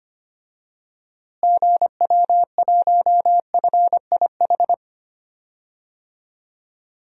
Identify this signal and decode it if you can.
Morse code, heard as a keyed tone, about 25 wpm, 700 Hz. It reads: GW1FIH
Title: DXCC-100 Call Signs - 6 Characters - Repeated 3x - 25wpm